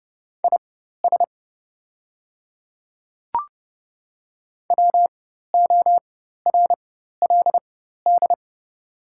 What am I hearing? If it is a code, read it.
Morse code, heard as a keyed tone, about 30 wpm, spaced as for 14 wpm, 700 Hz. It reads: IS WORLD
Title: 4x Character Spacing - Top 200 Words - 30wpm